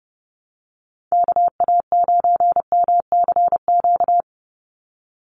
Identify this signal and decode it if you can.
Morse code, heard as a keyed tone, about 30 wpm, 700 Hz. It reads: KA9MCQ
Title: USA Call Signs 30wpm